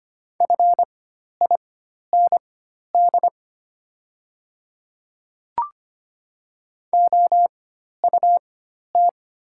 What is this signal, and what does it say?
Morse code, heard as a keyed tone, about 25 wpm, spaced as for 12 wpm, 700 Hz. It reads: FIND OUT
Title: 4x Character Spacing - Top 100 Words - 25wpm